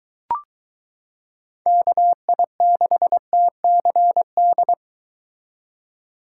Morse code, heard as a keyed tone, about 23 wpm, 700 Hz. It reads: KI6TCD